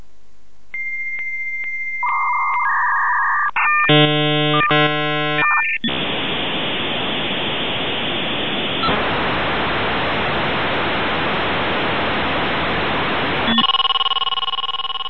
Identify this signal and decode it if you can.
Dial-up modem handshake.